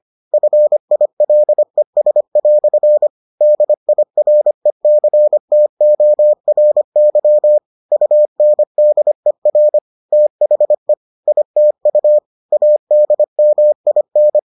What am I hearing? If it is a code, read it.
Morse code, heard as a keyed tone, about 25 wpm, 600 Hz. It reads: FILES" DIRECTORY UNDER THE ITU ADMIN